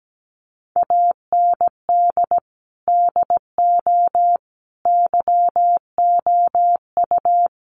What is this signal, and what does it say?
Morse code, heard as a keyed tone, about 17 wpm, 700 Hz. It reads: AND DO YOU
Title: Sentences from Top 100 Words 17wpm